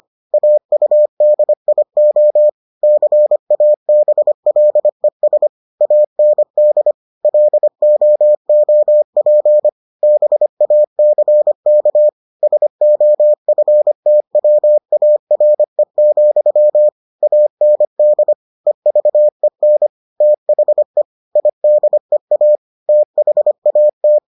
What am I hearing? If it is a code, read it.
Morse code, heard as a keyed tone, about 25 wpm, 600 Hz. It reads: AUDIO CABLES AND LOOP BACK SOFTWARE, AND EVEN THE IDEA THAT